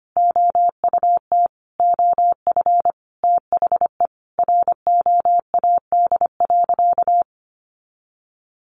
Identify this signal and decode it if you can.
Morse code, heard as a keyed tone, about 25 wpm, 700 Hz. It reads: OUT OF THE ROAD.
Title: Sentences from Top 400 Words 25wpm